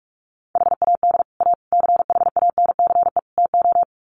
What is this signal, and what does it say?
Morse code, heard as a keyed tone, about 44 wpm, 700 Hz. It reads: HAD A CHANCE TO
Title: Sets of 4 Words 45wpm